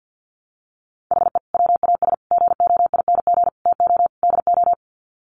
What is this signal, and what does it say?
Morse code, heard as a keyed tone, about 50 wpm, 700 Hz. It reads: HE WAS GOING TO DO
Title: Sets of 5 Words Encore2 50wpm